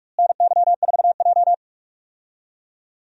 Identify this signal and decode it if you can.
Morse code, heard as a keyed tone, about 46 wpm, 700 Hz. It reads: NY4J
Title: USA Call Signs - 4 Characters - Repeated 3x - 45wpm